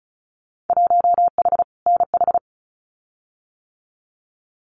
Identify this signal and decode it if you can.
Morse code, heard as a keyed tone, about 36 wpm, 700 Hz. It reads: 1H NH
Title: Winter Field Day Exchanges 35wpm